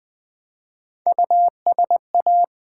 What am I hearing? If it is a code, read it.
Morse code, heard as a keyed tone, about 20 wpm, 700 Hz. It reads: USA
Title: Build A Word - 1 Letter At A Time - 3 to 5-Letter Words - 20wpm